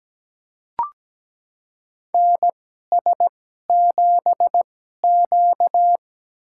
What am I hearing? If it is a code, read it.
Morse code, heard as a keyed tone, about 17 wpm, spaced as for 12 wpm, 700 Hz. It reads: NS7Q
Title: USA Call Signs - 4 Characters - 2x Character Spacing - 17wpm